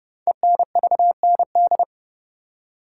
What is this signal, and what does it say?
Morse code, heard as a keyed tone, about 30 wpm, 700 Hz. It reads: ENVND
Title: Code Groups - 5 Letters - 30wpm